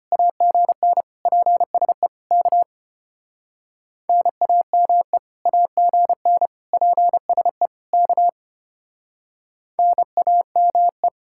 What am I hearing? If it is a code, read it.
Morse code, heard as a keyed tone, about 33 wpm, 700 Hz. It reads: AGN PSE K NAME AGN PSE K NAME